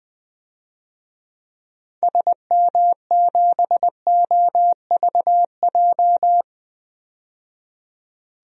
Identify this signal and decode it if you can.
Morse code, heard as a keyed tone, about 20 wpm, 700 Hz. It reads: SM7OVJ
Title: DXCC-100 Call Signs - 6 Characters - 20wpm